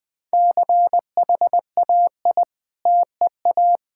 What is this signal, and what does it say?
Morse code, heard as a keyed tone, about 20 wpm, 700 Hz. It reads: CHAI TEA
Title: Difficult Words with Hints - 20wpm